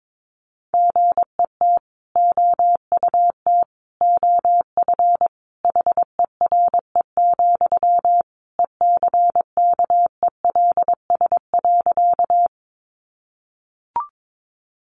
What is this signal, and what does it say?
Morse code, heard as a keyed tone, about 22 wpm, 700 Hz. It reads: GET OUT OF HERE, ECKELS.
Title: A Sound Of Thunder 22wpm